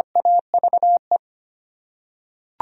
Morse code, heard as a keyed tone, about 25 wpm, 700 Hz. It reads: AVE
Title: Top 100 Words 25wpm